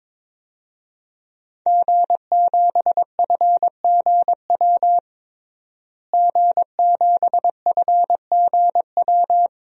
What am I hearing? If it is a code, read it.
Morse code, heard as a keyed tone, about 22 wpm, 700 Hz. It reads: G7FGW G7FGW
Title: DXCC-100 Call Signs - 5 Characters - Repeated 2x - 22wpm